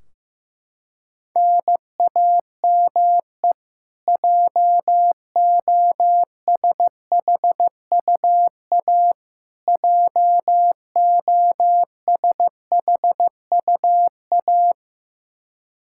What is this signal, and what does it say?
Morse code, heard as a keyed tone, about 15 wpm, 700 Hz. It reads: NAME JOSHUA JOSHUA
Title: Practice QSO - Exchange 1 - Beginning - 15wpm